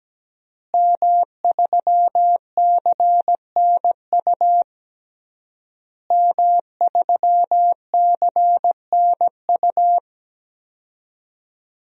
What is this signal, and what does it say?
Morse code, heard as a keyed tone, about 17 wpm, 700 Hz. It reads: M3CNU M3CNU
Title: DXCC-100 Call Signs - 5 Characters - Repeated 3x - 17wpm